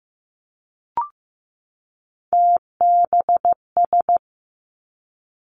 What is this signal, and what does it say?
Morse code, heard as a keyed tone, about 15 wpm, 700 Hz. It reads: TBS
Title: Three Challenging Letter-Number Combinations 15wpm